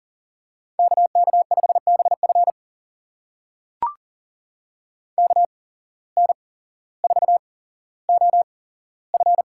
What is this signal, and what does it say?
Morse code, heard as a keyed tone, about 39 wpm, spaced as for 17 wpm, 700 Hz. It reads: KK5BF KN4OF
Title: USA Call Signs - 5 Characters - 8x Character Spacing - 40wpm